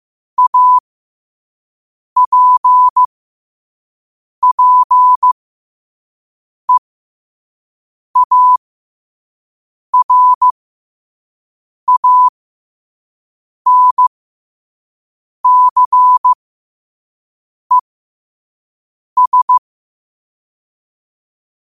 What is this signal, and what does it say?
Morse code, heard as a keyed tone, about 15 wpm, spaced as for 5 wpm, 1000 Hz. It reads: APPEARANCES
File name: qotdfiles / 5wpm